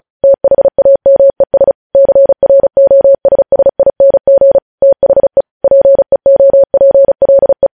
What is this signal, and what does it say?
Morse code, heard as a keyed tone, about 36 wpm, 550 Hz. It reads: THAMES CROSSING THE PEOPLE
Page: News headlines in Morse code at 35 wpm